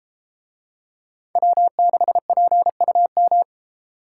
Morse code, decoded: W6PUM

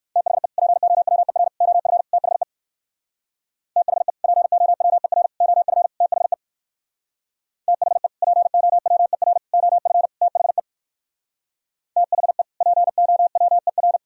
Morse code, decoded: THE POWER OF THE THE POWER OF THE THE POWER OF THE THE POWER

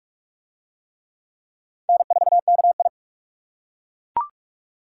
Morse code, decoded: N4KI

